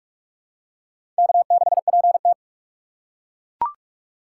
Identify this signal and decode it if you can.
Morse code, heard as a keyed tone, about 42 wpm, 700 Hz. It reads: K6PT
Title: USA Call Signs - 4 Characters - 2x Character Spacing - 45wpm